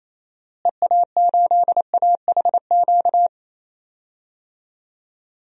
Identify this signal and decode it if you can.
Morse code, heard as a keyed tone, about 28 wpm, 700 Hz. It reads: EA8AHQ
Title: DXCC-100 Call Signs - 6 Characters - Repeated 3x - 28wpm